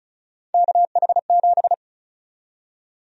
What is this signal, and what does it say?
Morse code, heard as a keyed tone, about 34 wpm, 700 Hz. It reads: KH7
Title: USA Call Sign Prefixes - 35wpm